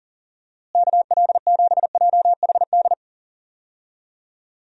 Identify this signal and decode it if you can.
Morse code, heard as a keyed tone, about 37 wpm, 700 Hz. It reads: KL7JHD
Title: USA Call Signs - 6 Characters - Repeated 3x - 40wpm